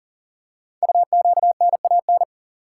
Morse code, heard as a keyed tone, about 44 wpm, 700 Hz. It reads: UQNAN
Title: Code Groups - 5 Letters - 40wpm